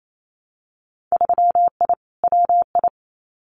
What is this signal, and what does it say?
Morse code, heard as a keyed tone, about 28 wpm, 700 Hz. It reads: 3I WI
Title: Winter Field Day Exchanges 28wpm